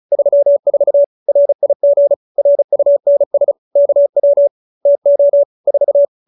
Morse code, heard as a keyed tone, about 34 wpm, 570 Hz. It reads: 34 RIG RUNS KW TO 4